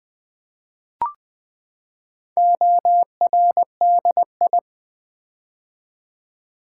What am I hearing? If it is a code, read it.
Morse code, heard as a keyed tone, about 20 wpm, 700 Hz. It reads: ORDI
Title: Build A Word - 4 Letters At A Time - 8-Letter Words - 20wpm